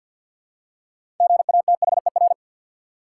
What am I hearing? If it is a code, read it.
Morse code, heard as a keyed tone, about 47 wpm, 700 Hz. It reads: GATHER
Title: Six Letter Words - 50wpm